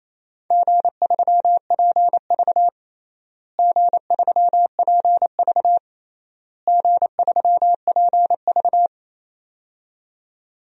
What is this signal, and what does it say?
Morse code, heard as a keyed tone, about 28 wpm, 700 Hz. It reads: G3PV G3PV G3PV
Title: DXCC-100 Call Signs - 4 Characters - Repeated 3x - 28wpm